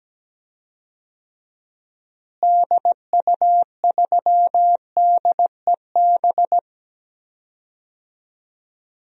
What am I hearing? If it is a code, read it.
Morse code, heard as a keyed tone, about 17 wpm, 700 Hz. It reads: DU3DEB
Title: DXCC-100 Call Signs - 6 Characters - Repeated 3x - 17wpm